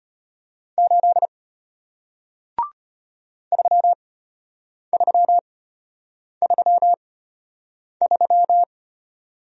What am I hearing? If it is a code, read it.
Morse code, heard as a keyed tone, about 34 wpm, spaced as for 8 wpm, 700 Hz. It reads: 8 3333